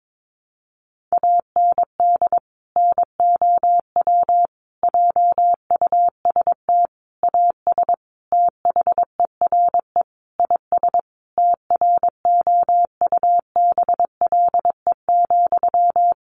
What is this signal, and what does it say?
Morse code, heard as a keyed tone, about 22 wpm, 700 Hz. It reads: AND NOW JUST AS THERE IS TROUBLE,
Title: Sentences from Top 1200 Words 22wpm